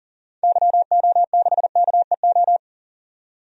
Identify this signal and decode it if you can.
Morse code, heard as a keyed tone, about 38 wpm, 700 Hz. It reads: YO6KEO
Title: DXCC-100 Call Signs - 6 Characters - 40wpm